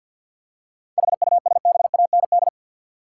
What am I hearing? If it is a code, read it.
Morse code, heard as a keyed tone, about 46 wpm, 700 Hz. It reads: HUSBAND
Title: Seven Letter Words - 50wpm